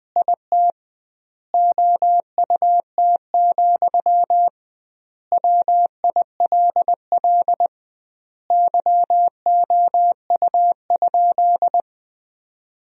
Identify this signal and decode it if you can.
Morse code, heard as a keyed tone, about 21 wpm, 700 Hz. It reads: IT OUT, WILL YOU?
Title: 2x Word-Spacing - Sentences from Top 300 Words - 20wpm